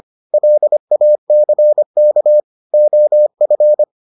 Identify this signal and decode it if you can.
Morse code, heard as a keyed tone, about 25 wpm, 600 Hz. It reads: LACK OF